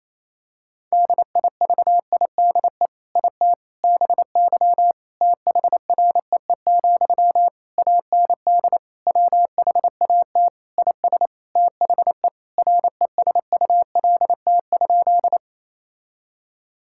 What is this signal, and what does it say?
Morse code, heard as a keyed tone, about 28 wpm, 700 Hz. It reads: DIVIDE IT BY THREE, AND WHAT IS THE RESULT?
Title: Sentences from Top 1600 Words 28wpm